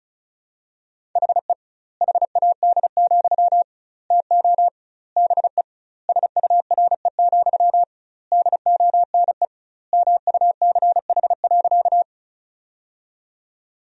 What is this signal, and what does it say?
Morse code, heard as a keyed tone, about 36 wpm, 700 Hz. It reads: HE HAD, TO BE SURE, DONE MUCH.
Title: 2x Word-Spacing - Sentences from Top 500 Words - 35wpm